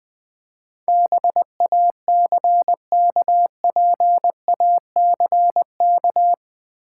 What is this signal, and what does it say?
Morse code, heard as a keyed tone, about 20 wpm, 700 Hz. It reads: BACKPACK
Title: Easy Words with Hints - 20wpm